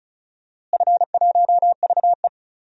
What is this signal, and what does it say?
Morse code, heard as a keyed tone, about 34 wpm, 700 Hz. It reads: F1VE
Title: DXCC-100 Call Signs - 4 Characters - Repeated 2x - 35wpm